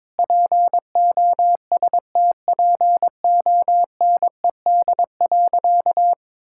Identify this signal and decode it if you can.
Morse code, heard as a keyed tone, about 22 wpm, 700 Hz. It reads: POSTPONED.
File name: practice / 22wpm / Sentences with 8-Word Phrases - 22wpm